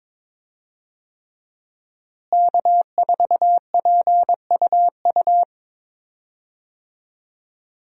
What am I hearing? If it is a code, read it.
Morse code, heard as a keyed tone, about 22 wpm, 700 Hz. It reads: K4PUU
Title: USA Call Signs - 5 Characters - Repeated 3x - 22wpm